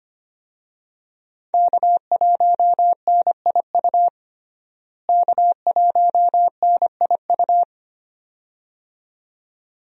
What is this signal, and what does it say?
Morse code, heard as a keyed tone, about 25 wpm, 700 Hz. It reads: K1NIU K1NIU
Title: USA Call Signs - 5 Characters - Repeated 2x - 25wpm